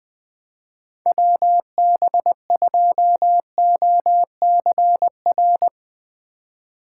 Morse code, heard as a keyed tone, about 20 wpm, 700 Hz. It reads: WB2OCR